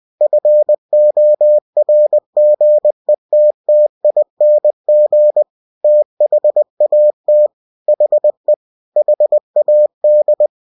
Morse code, heard as a keyed tone, about 20 wpm, 600 Hz. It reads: FORGETTING THAT HE HAD
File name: cw-books / winnie-the-pooh / winnie-20wpm-page-0057